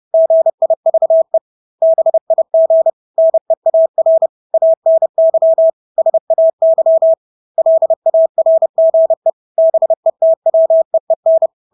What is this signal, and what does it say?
Morse code, heard as a keyed tone, about 30 wpm, 650 Hz. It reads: GIVE BIG NEAR ANY SAY LARGE BETWEEN
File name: Engelske ord 30-30 wpm